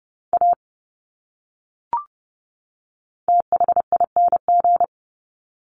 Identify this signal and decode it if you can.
Morse code, heard as a keyed tone, about 30 wpm, 700 Hz. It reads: A THING
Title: Top 200 Words 30wpm